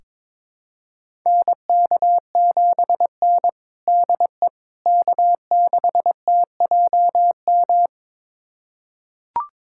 Morse code, heard as a keyed tone, about 22 wpm, 700 Hz. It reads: NK7N DE K6TJM
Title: USA Callsign DE Callsign 22wpm